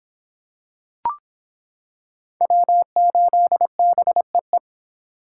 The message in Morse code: W8BEE

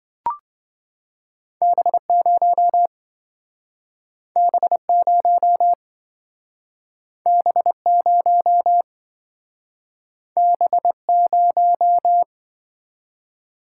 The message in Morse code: B0 B0 B0 B0

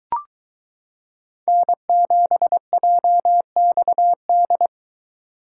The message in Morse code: N7JXD

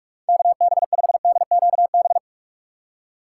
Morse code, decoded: KB5DQB

